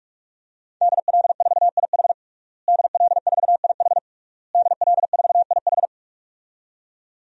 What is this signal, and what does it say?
Morse code, heard as a keyed tone, about 46 wpm, 700 Hz. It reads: DL4IH DL4IH DL4IH
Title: DXCC-100 Call Signs - 5 Characters - Repeated 3x - 45wpm